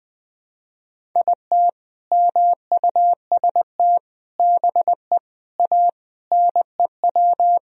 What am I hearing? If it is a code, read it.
Morse code, heard as a keyed tone, about 20 wpm, 700 Hz. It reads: IT MUST BE A NEW
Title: Sentences from Top 500 Words and Abbreviations 20wpm